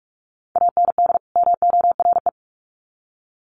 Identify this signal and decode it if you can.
Morse code, heard as a keyed tone, about 44 wpm, 700 Hz. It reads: AND MORE
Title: Sets of 2 Words 45wpm